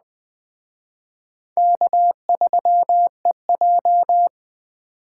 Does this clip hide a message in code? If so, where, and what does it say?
Yes: Morse code from 1.6 to 4.3 s, heard as a keyed tone, about 20 wpm, 700 Hz. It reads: K3EJ